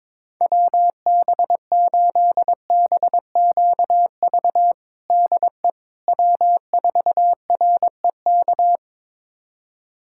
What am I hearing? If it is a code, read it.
Morse code, heard as a keyed tone, about 22 wpm, 700 Hz. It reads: WB8BQV DE W4REK
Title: USA Callsign DE Callsign 22wpm